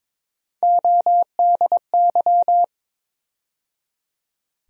Morse code, heard as a keyed tone, about 22 wpm, 700 Hz. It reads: ODY
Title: USA Call Sign Postfixes - 3 Letters 22wpm